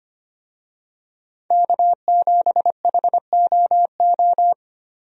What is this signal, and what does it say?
Morse code, heard as a keyed tone, about 25 wpm, 700 Hz. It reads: K7HOO
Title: USA Call Signs - 5 Characters - Repeated 3x - 25wpm